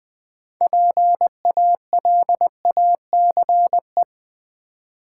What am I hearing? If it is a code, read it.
Morse code, heard as a keyed tone, about 20 wpm, 700 Hz. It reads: PALACE
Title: Build A Word - 3 Letters At A Time - 6-Letter Words - 20wpm